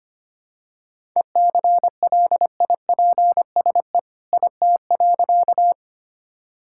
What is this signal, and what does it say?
Morse code, heard as a keyed tone, about 25 wpm, 700 Hz. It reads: ECLIPSE IT.
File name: practice / 25wpm / All-Age Appropriate Jokes - 25wpm